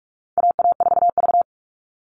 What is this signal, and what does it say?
Morse code, heard as a keyed tone, about 44 wpm, 700 Hz. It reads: AA4V
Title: All Cut Numbers - 4-Digit 45wpm